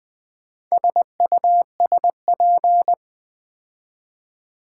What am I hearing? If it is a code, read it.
Morse code, heard as a keyed tone, about 20 wpm, 700 Hz. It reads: SUSP
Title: Build A Word - 4 Letters At A Time - 10-Letter Words - 20wpm